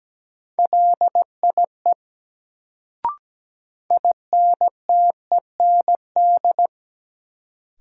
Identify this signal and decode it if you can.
Morse code, heard as a keyed tone, about 17 wpm, 700 Hz. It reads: LIE INTEND
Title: Sentences from Top 1600 Words - New Word Drill - 17wpm